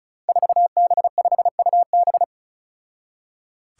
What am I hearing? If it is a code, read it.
Morse code, heard as a keyed tone, about 34 wpm, 700 Hz. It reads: 4B5UB
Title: Five Challenging Letter-Number Combinations 35wpm